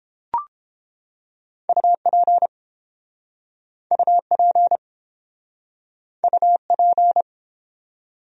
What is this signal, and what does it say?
Morse code, heard as a keyed tone, about 30 wpm, 700 Hz. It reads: UP UP UP